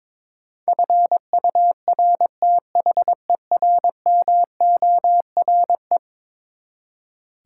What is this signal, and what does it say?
Morse code, heard as a keyed tone, about 22 wpm, 700 Hz. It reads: FURTHERMORE
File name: practice / 22wpm / Eleven Letter Words - 22wpm